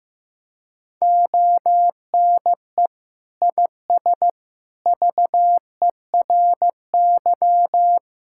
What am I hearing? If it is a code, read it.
Morse code, heard as a keyed tone, about 15 wpm, 700 Hz. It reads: ONE IS VERY
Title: Sentences from Top 1000 Words - Review Part 5 - 15wpm